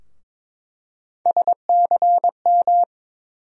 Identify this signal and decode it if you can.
Morse code, heard as a keyed tone, about 22 wpm, 700 Hz. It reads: SCM
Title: USA Call Sign Postfixes - 3 Letters 22wpm